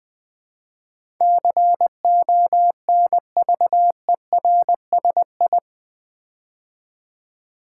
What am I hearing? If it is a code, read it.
Morse code, heard as a keyed tone, about 20 wpm, 700 Hz. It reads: CONVERSI
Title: Build A Word - 4 Letters At A Time - 10-Letter Words - 20wpm